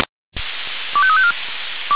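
Five-tone selective call (ZVEI), heard as quick runs of five tones, tones of about 70 ms. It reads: 25354
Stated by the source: ZVEI five-tone call received on SDR.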